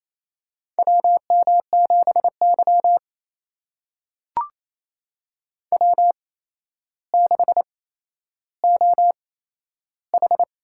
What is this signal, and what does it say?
Morse code, heard as a keyed tone, about 28 wpm, spaced as for 12 wpm, 700 Hz. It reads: WM7Y W6OH
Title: USA Call Signs - 4 Characters - 8x Character Spacing - 28wpm